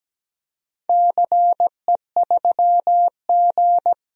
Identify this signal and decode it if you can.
Morse code, heard as a keyed tone, about 17 wpm, 700 Hz. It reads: CE3G
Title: DXCC-100 Call Signs - 4 Characters - 17wpm